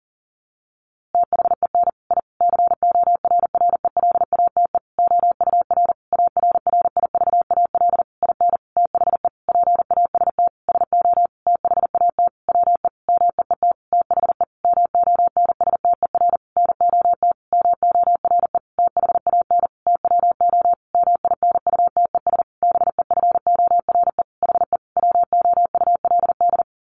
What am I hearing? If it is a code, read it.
Morse code, heard as a keyed tone, about 40 wpm, 700 Hz. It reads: THEN I CORRELATE OUR ARRIVAL IN THE PAST SO THAT WE MEET THE MONSTER NOT MORE THAN TWO MINUTES BEFORE HE WOULD
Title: A Sound Of Thunder 40wpm